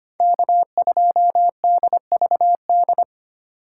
Morse code, heard as a keyed tone, about 25 wpm, 700 Hz. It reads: K2DVD
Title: USA Call Signs - 5 Characters - 3x Character Spacing - 25wpm